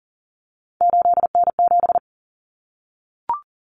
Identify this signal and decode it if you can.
Morse code, heard as a keyed tone, about 40 wpm, 700 Hz. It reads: 8N7
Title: Common Cut Numbers - 3-Digit 40wpm